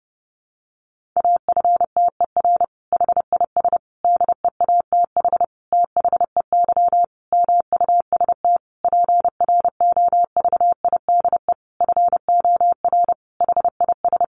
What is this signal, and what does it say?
Morse code, heard as a keyed tone, about 30 wpm, 700 Hz. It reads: AFTER HIS DEATH THEY MUST PROVIDE FOR HIS
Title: Sentences from Top 1100 Words 30wpm